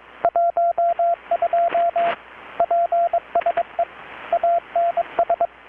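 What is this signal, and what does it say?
Morse code, heard as a keyed tone, about 22 wpm, 680 Hz, with radio noise. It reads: 12 PSE ANS
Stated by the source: Morse from KPH's Night of Nights XX maritime radiotelegraphy event, received in Chiapas, Mexico about 3,800 km away on a Kenwood R-600 with a grounded 10 m long-wire antenna.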